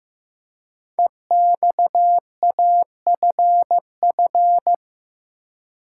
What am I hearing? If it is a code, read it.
Morse code, heard as a keyed tone, about 15 wpm, 700 Hz. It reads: EXAFF
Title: Code Groups - 5 Letters - 15wpm